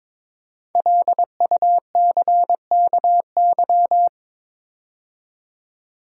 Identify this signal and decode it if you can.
Morse code, heard as a keyed tone, about 22 wpm, 700 Hz. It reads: LUCKY
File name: 5 Letter Words - First 3 Letters Given - 22wpm